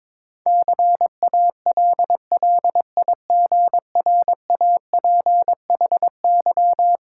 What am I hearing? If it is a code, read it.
Morse code, heard as a keyed tone, about 22 wpm, 700 Hz. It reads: CALLIGRAPHY
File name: Difficult Words with Hints - 22wpm